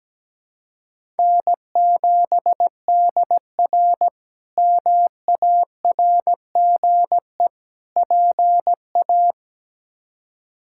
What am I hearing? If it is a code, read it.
Morse code, heard as a keyed tone, about 17 wpm, 700 Hz. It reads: N7DR MARGE PA
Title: K1USN SST QSO Practice - 17wpm